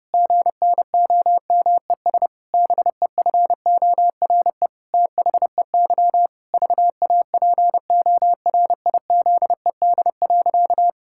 Morse code, heard as a keyed tone, about 30 wpm, 700 Hz. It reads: GNOMES BEFORE THEY VAPORIZED.
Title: Pangram Challenge v2 - 30wpm